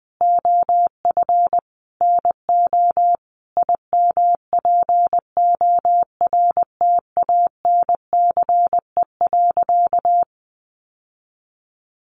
Morse code, heard as a keyed tone, about 20 wpm, 700 Hz. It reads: OF NO IMPORTANCE.
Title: Sentences from Top 1200 Words 20wpm